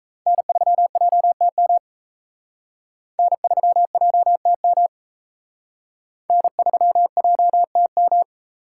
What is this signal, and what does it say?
Morse code, heard as a keyed tone, about 36 wpm, 700 Hz. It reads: N3JTM N3JTM N3JTM